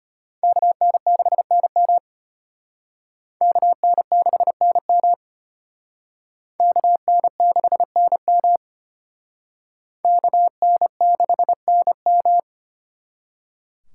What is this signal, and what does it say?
Morse code, heard as a keyed tone, about 31 wpm, 700 Hz. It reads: KN6NM KN6NM KN6NM KN6NM
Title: Speed Racing - USA 2x2 Call Signs 25wpm